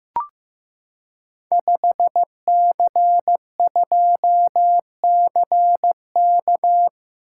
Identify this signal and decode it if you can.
Morse code, heard as a keyed tone, about 15 wpm, 700 Hz. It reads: 5C2CK
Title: Code Groups - 5 Letter:Numbers - 15wpm